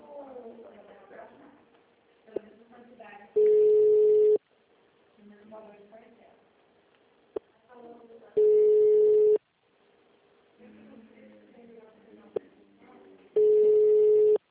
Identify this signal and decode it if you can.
Ringback tone (Europe, 425 Hz), with speech.